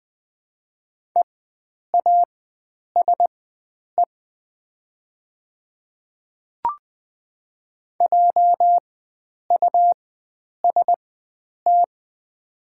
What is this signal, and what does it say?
Morse code, heard as a keyed tone, about 20 wpm, spaced as for 9 wpm, 700 Hz. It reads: EASE JUST
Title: 4x Character Spacing - Top 300 Words - 20wpm